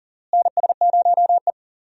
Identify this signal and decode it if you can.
Morse code, heard as a keyed tone, about 40 wpm, 700 Hz. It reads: NS0E
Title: USA Call Signs - 4 Characters - Repeated 2x - 40wpm